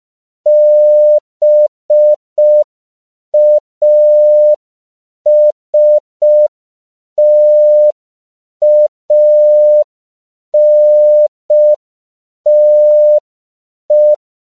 Morse code, heard as a keyed tone, about 5 wpm, 600 Hz. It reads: BASTANTE
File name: CW / 5PPM